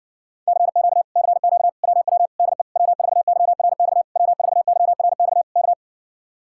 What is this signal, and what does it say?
Morse code, heard as a keyed tone, about 69 wpm, 700 Hz. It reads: CQ CQ WW DE W4YRX W4YRX K